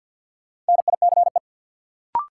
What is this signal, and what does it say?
Morse code, heard as a keyed tone, about 50 wpm, 700 Hz. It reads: NICE
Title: Sentences from Top 1100 Words - New Word Drill - 50wpm